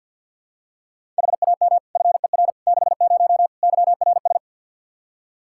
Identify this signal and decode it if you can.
Morse code, heard as a keyed tone, about 48 wpm, 700 Hz. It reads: HAM FER 60 YRS
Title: K1USN SST QSO Practice 50wpm